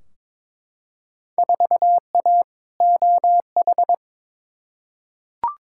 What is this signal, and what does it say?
Morse code, heard as a keyed tone, about 22 wpm, 700 Hz. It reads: 4A OH
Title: Field Day Exchanges - 22wpm